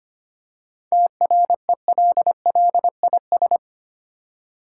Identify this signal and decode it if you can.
Morse code, heard as a keyed tone, about 27 wpm, 700 Hz. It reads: TRELLIS
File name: Easy Words - Speed-Racing with Hints - 25wpm